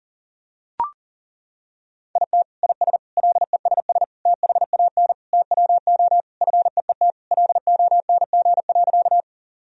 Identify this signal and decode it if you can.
Morse code, heard as a keyed tone, about 40 wpm, 700 Hz. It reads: IT IS LESS THAN TWO FEET LONG.
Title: Sentences from Top 1000 Words - Review Part 2 - 40wpm